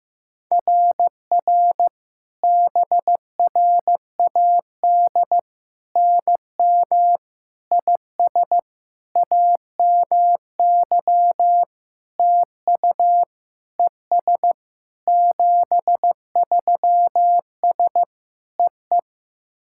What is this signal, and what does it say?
Morse code, heard as a keyed tone, about 15 wpm, 700 Hz. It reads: RR BRAD NM IS AMY TU ES 73S EE
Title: SOTA QSO Practice 15wpm